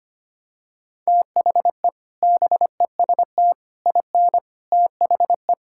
Morse code, heard as a keyed tone, about 25 wpm, 700 Hz. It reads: THE BEST IN THE
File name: practice / 25wpm / Sentences with 3-Word Phrases - 25wpm